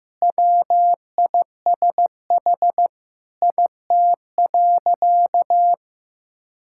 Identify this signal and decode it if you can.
Morse code, heard as a keyed tone, about 16 wpm, 700 Hz. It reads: WISH IT.
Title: Sentences from Top 1600 Words 15wpm